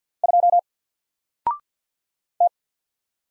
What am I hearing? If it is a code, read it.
Morse code, heard as a keyed tone, about 48 wpm, 700 Hz. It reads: 2 T
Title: Single Character (Letter:Number:Punctuation:Prosign) Alt 50wpm